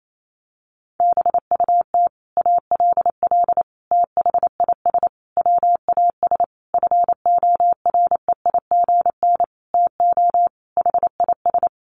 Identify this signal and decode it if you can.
Morse code, heard as a keyed tone, about 28 wpm, 700 Hz. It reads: BUT ALL THIS WAS FOREIGN TO HIS
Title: Sentences from Top 1400 Words 28wpm